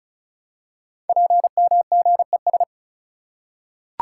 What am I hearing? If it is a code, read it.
Morse code, heard as a keyed tone, about 36 wpm, 700 Hz. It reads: PMGES
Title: Code Groups - 5 Letters - 35wpm